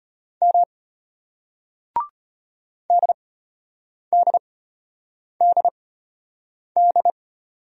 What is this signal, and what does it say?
Morse code, heard as a keyed tone, about 36 wpm, spaced as for 8 wpm, 700 Hz. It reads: M DDDD